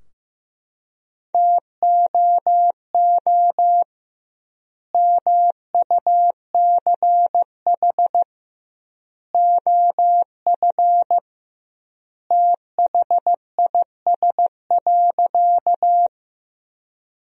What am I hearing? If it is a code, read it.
Morse code, heard as a keyed tone, about 16 wpm, 700 Hz. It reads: TOO MUCH OF THIS.
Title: 2x Word-Spacing - Sentences from Top 200 Words - 15wpm